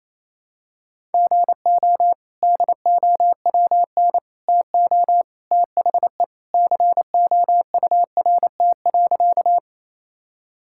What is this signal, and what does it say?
Morse code, heard as a keyed tone, about 28 wpm, 700 Hz. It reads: GO DOWN TO THE COURT.
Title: Sentences from Top 1600 Words 28wpm